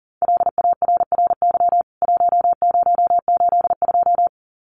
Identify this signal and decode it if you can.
Morse code, heard as a keyed tone, about 40 wpm, 700 Hz. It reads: LARRY 1082
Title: CWT Exchange 40wpm